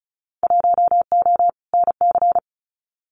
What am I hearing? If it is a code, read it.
Morse code, heard as a keyed tone, about 34 wpm, 700 Hz. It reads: 1O NC
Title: Winter Field Day Exchanges 35wpm